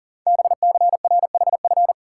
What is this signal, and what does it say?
Morse code, heard as a keyed tone, about 40 wpm, 700 Hz. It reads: BCRHF
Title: Code Groups - 5 Letters - 40wpm